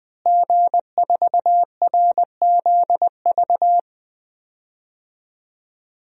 Morse code, heard as a keyed tone, about 20 wpm, 700 Hz. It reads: G4RZV
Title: DXCC-100 Call Signs - 5 Characters - Repeated 3x - 20wpm